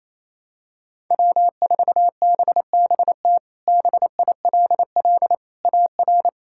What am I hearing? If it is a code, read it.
Morse code, heard as a keyed tone, about 28 wpm, 700 Hz. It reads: W4BBT BILL AR
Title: K1USN SST QSO Practice - 28wpm